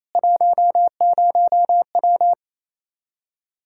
Morse code, heard as a keyed tone, about 28 wpm, 700 Hz. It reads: 10W
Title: Practice QSO - Exchange 2 - Info - 28wpm